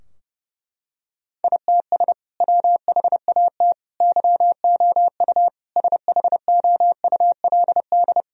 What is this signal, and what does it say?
Morse code, heard as a keyed tone, about 30 wpm, 700 Hz. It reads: ITS WHAT YOU SHOULD
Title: Sentences from Top 1000 Words - Review Part 3 - 30wpm